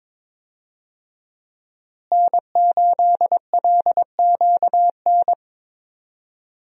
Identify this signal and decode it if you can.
Morse code, heard as a keyed tone, about 22 wpm, 700 Hz. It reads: N8LQN
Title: USA Call Signs - 5 Characters - Repeated 2x - 22wpm